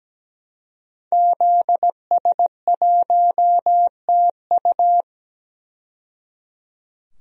Morse code, heard as a keyed tone, about 17 wpm, 700 Hz. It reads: ZS1TU
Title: DXCC-100 Call Signs - 5 Characters - 17wpm